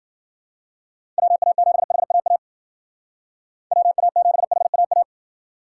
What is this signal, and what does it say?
Morse code, heard as a keyed tone, about 51 wpm, 700 Hz. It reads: WA7HAA WA7HAA